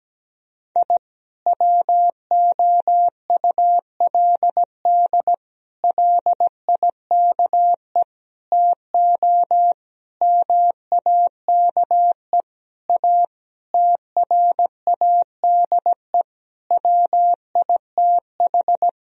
Morse code, heard as a keyed tone, about 17 wpm, 700 Hz. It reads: I WOULD LIKE TO MAKE A TRADE WITH
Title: Sentences from Top 1000 Words - Review Part 4 - 17wpm